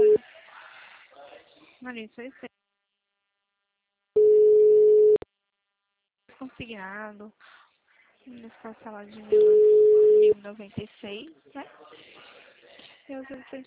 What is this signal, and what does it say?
Ringback tone (Europe, 425 Hz), with speech.
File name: out- -2017- - - .39684